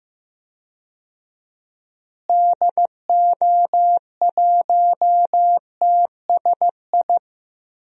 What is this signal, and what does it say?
Morse code, heard as a keyed tone, about 15 wpm, 700 Hz. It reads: DO1TSI
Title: DXCC-100 Call Signs - 6 Characters - Repeated 3x - 15wpm